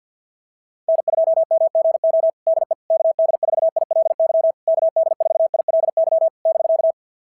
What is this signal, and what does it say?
Morse code, heard as a keyed tone, about 46 wpm, 660 Hz. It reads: N2MGO DE KD4ILY KD4ILY <BK>
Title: SOTA QSO Practice 50wpm